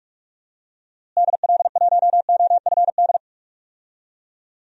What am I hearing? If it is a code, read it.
Morse code, heard as a keyed tone, about 43 wpm, 700 Hz. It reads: DL1OFD
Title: DXCC-100 Call Signs - 6 Characters - Repeated 3x - 45wpm